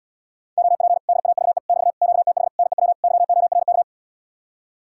Morse code, heard as a keyed tone, about 80 wpm, 700 Hz. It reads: OP DAVE <BT> QTH NEW YORK